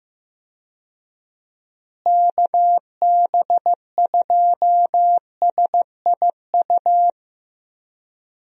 Morse code, heard as a keyed tone, about 15 wpm, 700 Hz. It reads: KB2SIU
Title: USA Call Signs - 6 Characters - Repeated 2x - 15wpm